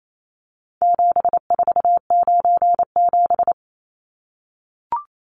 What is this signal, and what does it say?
Morse code, heard as a keyed tone, about 28 wpm, 700 Hz. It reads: 7497
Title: Numbers - 4-Digit 28wpm